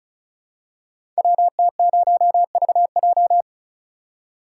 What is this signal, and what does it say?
Morse code, heard as a keyed tone, about 36 wpm, 700 Hz. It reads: WT0VJ
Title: Code Groups - 5 Letter:Numbers - 35wpm